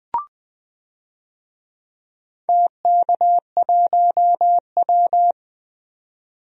Morse code, heard as a keyed tone, about 20 wpm, 700 Hz. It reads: TK1W
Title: DXCC-100 Call Signs - 4 Characters - 20wpm